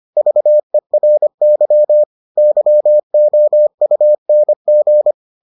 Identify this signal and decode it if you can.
Morse code, heard as a keyed tone, about 25 wpm, 600 Hz. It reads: VERY YOUNG